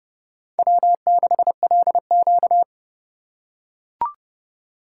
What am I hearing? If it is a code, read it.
Morse code, heard as a keyed tone, about 30 wpm, 700 Hz. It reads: W6LQ
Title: USA Call Signs - 4 Characters - 2x Character Spacing - 30wpm